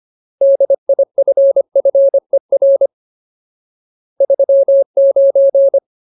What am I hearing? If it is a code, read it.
Morse code, heard as a keyed tone, about 25 wpm, 550 Hz. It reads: DIFFER 39
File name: Ultimate Word List / 25WPM Ultimate